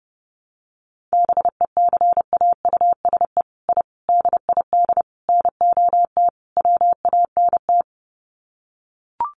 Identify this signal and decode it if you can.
Morse code, heard as a keyed tone, about 30 wpm, 700 Hz. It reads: BECAUSE I DID NOT WANT
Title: Sets of 5 Words Encore2 30wpm